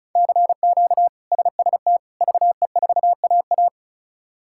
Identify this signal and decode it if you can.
Morse code, heard as a keyed tone, about 36 wpm, 700 Hz. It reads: CQ SST VE4AA
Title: K1USN SST QSO Practice - 35wpm